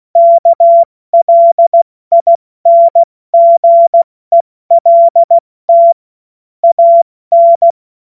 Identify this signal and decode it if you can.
Morse code, heard as a keyed tone, about 16 wpm, 680 Hz. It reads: KLINGELT AN
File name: Morsekurs / Witze / Witz 24. Januar 2023 70 F80